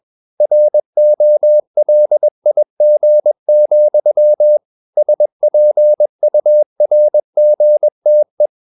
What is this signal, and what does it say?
Morse code, heard as a keyed tone, about 21 wpm, 600 Hz. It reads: ROLIG, SPURGTE
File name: Ferie QRS 21wpm